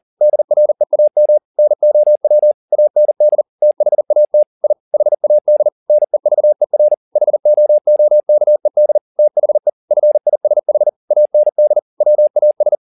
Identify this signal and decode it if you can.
Morse code, heard as a keyed tone, about 40 wpm, 600 Hz. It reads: DREAM NOW AND THAT I HAD NEVER HOOKED THE FISH AND WAS
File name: the old man and the sea 2 40wpm